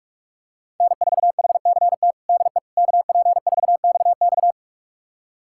Morse code, heard as a keyed tone, about 46 wpm, 700 Hz. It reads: N4HCT DE KP4XX
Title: USA Callsign DE Callsign - 5 Characters - 45wpm